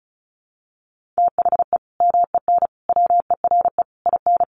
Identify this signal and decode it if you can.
Morse code, heard as a keyed tone, about 34 wpm, 700 Hz. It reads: THE MEN WERE IN
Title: Sentences from Top 1400 Words 35wpm